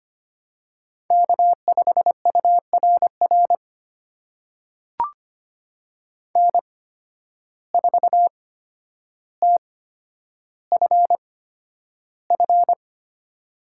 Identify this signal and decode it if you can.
Morse code, heard as a keyed tone, about 25 wpm, spaced as for 11 wpm, 700 Hz. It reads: K5URR N4TFF
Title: USA Call Signs - 5 Characters - 8x Character Spacing - 25wpm